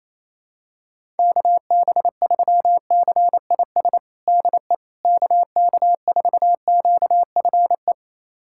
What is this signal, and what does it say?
Morse code, heard as a keyed tone, about 28 wpm, 700 Hz. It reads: KB3CIS DE KK4QFE
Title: USA Callsign DE Callsign - 6 Characters - 28wpm